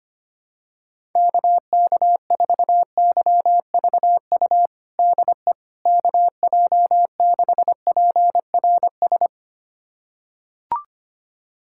Morse code, heard as a keyed tone, about 25 wpm, 700 Hz. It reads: KK4YVU DE KJ6PRS
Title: USA Callsign DE Callsign 25wpm